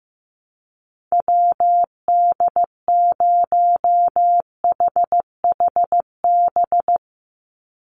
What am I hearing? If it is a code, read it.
Morse code, heard as a keyed tone, about 15 wpm, 700 Hz. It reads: WD0HHB
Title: USA Call Signs 15wpm